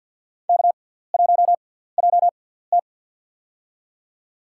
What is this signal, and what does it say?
Morse code, heard as a keyed tone, about 53 wpm, spaced as for 17 wpm, 700 Hz. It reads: K1JT
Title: USA Call Signs - 4 Characters - 6x Character Spacing - 50wpm